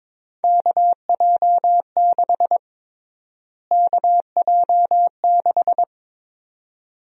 Morse code, heard as a keyed tone, about 22 wpm, 700 Hz. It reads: KJ6 KJ6
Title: USA Call Sign Prefixes - Repeated 2x - 22wpm